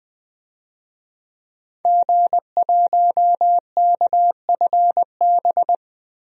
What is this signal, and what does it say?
Morse code, heard as a keyed tone, about 20 wpm, 700 Hz. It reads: G1KFB
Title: DXCC-100 Call Signs - 5 Characters - 20wpm